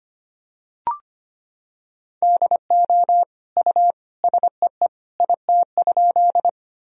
Morse code, heard as a keyed tone, about 25 wpm, 700 Hz. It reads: DO U SEE IT?
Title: Sentences from Top 100 Words and Abbreviations 25wpm